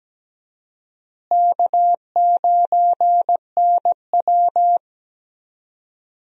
Morse code, heard as a keyed tone, about 17 wpm, 700 Hz. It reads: K9NW
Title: USA Call Signs - 4 Characters - Repeated 2x - 17wpm